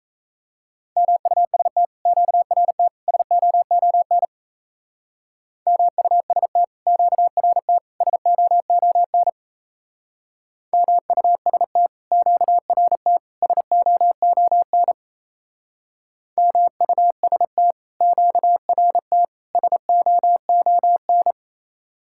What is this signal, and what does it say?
Morse code, heard as a keyed tone, about 37 wpm, 700 Hz. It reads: MUST QRT SOON MUST QRT SOON MUST QRT SOON MUST QRT SOON